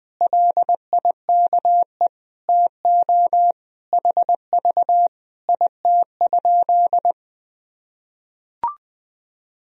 Morse code, heard as a keyed tone, about 20 wpm, 700 Hz. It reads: LIKE TO HV IT?
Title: Sentences from Top 100 Words and Abbreviations 20wpm